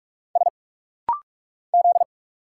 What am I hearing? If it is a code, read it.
Morse code, heard as a keyed tone, about 44 wpm, 700 Hz. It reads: S Z
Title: Single Letters - Rapid-Fire 45wpm